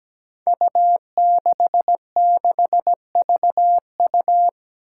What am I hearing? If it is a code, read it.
Morse code, heard as a keyed tone, about 17 wpm, 700 Hz. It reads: U66VU
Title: Five Challenging Letter-Number Combinations 17wpm